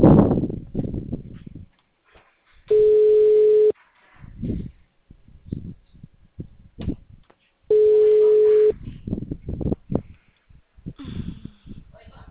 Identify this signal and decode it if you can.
Ringback tone (Europe, 425 Hz).
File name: out- -2039- - - .67852